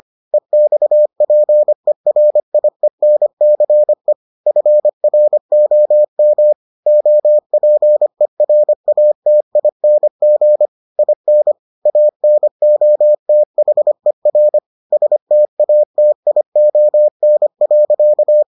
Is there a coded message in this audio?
Yes: Morse code from 0.3 to 18.4 s, heard as a keyed tone, about 25 wpm, 600 Hz. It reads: EXPERIENCE FROM OPERATING IN ANOTHER STATION.